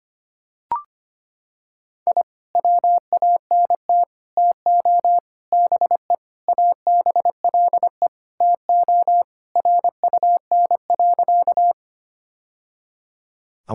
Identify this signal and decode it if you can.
Morse code, heard as a keyed tone, about 25 wpm, 700 Hz. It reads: I WANT TO BE ABLE TO RUN.
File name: practice / 25wpm / Sentences with 3-Word Phrases - 25wpm